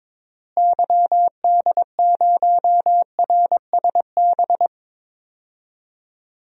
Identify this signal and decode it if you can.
Morse code, heard as a keyed tone, about 21 wpm, 700 Hz. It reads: YD0RSB
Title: DXCC-100 Call Signs - 6 Characters - Repeated 3x - 22wpm